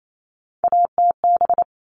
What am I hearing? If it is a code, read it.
Morse code, heard as a keyed tone, about 28 wpm, 700 Hz. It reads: ATB
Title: All Cut Numbers - 3-Digit 28wpm